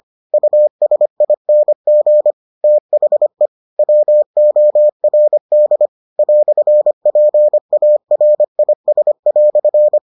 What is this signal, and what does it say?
Morse code, heard as a keyed tone, about 25 wpm, 600 Hz. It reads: USING THE WORD "PARIS"